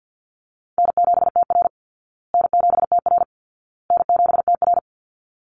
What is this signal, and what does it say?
Morse code, heard as a keyed tone, about 50 wpm, 700 Hz. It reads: N7TR N7TR N7TR
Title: Call Signs Repeated 2x - 50wpm